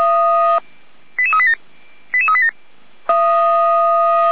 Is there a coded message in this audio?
Yes: five-tone selective call (ZVEI) from 1.2 to 2.5 s, heard as quick runs of five tones, tones of about 80 ms. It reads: 88387 88387